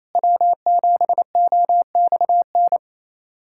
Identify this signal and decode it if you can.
Morse code, heard as a keyed tone, about 28 wpm, 700 Hz. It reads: W7OXN
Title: USA Callsign DE Callsign - 5 Characters - 28wpm